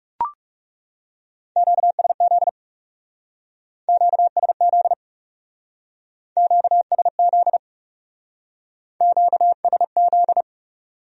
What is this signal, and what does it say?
Morse code, heard as a keyed tone, about 36 wpm, 700 Hz. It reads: QSZ QSZ QSZ QSZ
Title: Speed-Racing Q-Codes - 30wpm